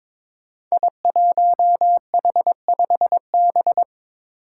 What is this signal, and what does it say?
Morse code, heard as a keyed tone, about 22 wpm, 700 Hz. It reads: I1H5B
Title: Five Challenging Letter-Number Combinations 22wpm